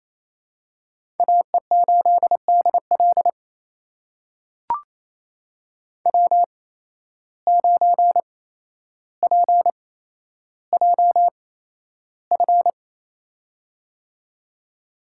Morse code, heard as a keyed tone, about 28 wpm, spaced as for 12 wpm, 700 Hz. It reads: AE8DL W9PJF
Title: USA Call Signs - 5 Characters - 8x Character Spacing - 28wpm